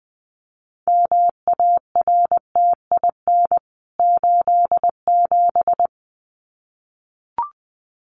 Morse code, heard as a keyed tone, about 20 wpm, 700 Hz. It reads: MARTIN 87
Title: CWT Exchange 20wpm